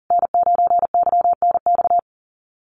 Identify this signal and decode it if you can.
Morse code, heard as a keyed tone, about 40 wpm, 700 Hz. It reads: N9YNX
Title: USA Call Signs Repeated 2x 40wpm